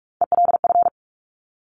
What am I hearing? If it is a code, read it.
Morse code, heard as a keyed tone, about 43 wpm, 700 Hz. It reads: ELF
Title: Top Three Letter Combinations in Words 45wpm